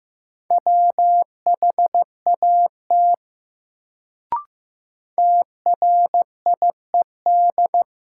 Morse code, heard as a keyed tone, about 15 wpm, 700 Hz. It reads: WHAT TRIED
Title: Sets of 2 Words - New Word Drill - 15wpm